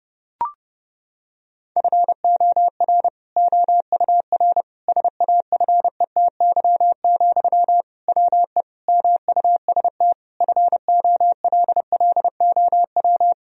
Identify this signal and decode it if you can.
Morse code, heard as a keyed tone, about 30 wpm, 700 Hz. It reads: FOR OUR SAFETY, WE MUST FOLLOW
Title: Sentences from Top 1000 Words - Review Part 4 - 30wpm